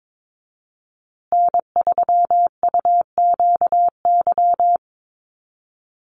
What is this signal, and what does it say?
Morse code, heard as a keyed tone, about 22 wpm, 700 Hz. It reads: N3UQY
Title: USA Call Signs Repeated 3x 22wpm